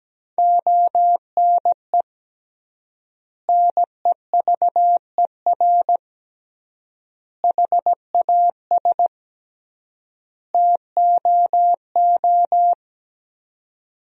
Morse code, heard as a keyed tone, about 17 wpm, 700 Hz. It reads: ONE NEVER HAS TOO
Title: 3x Word-Spacing - Sentences from Top 300 Words - 17wpm